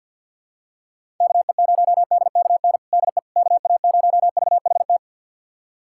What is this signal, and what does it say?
Morse code, heard as a keyed tone, about 46 wpm, 700 Hz. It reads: KE0DKN DE KA0VHT
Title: USA Callsign DE Callsign - 6 Characters - 50wpm